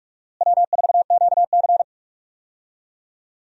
Morse code, heard as a keyed tone, about 41 wpm, 700 Hz. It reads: W4QC
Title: USA Call Signs - 4 Characters - Repeated 2x - 45wpm